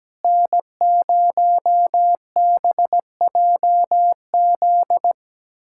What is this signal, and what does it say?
Morse code, heard as a keyed tone, about 17 wpm, 700 Hz. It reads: N0BJZ
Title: USA Callsign DE Callsign - 5 Characters - 17wpm